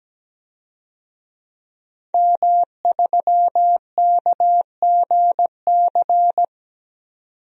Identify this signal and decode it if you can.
Morse code, heard as a keyed tone, about 17 wpm, 700 Hz. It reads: M3KGC
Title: DXCC-100 Call Signs - 5 Characters - 17wpm